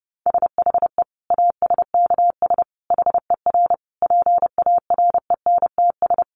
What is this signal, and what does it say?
Morse code, heard as a keyed tone, about 30 wpm, 700 Hz. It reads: SHE ASKS HER PARENTS
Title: Easy Phrases 30wpm